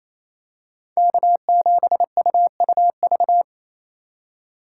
Morse code, heard as a keyed tone, about 28 wpm, 700 Hz. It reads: K7UUV
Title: USA Call Signs - 2x Character Spacing - 28wpm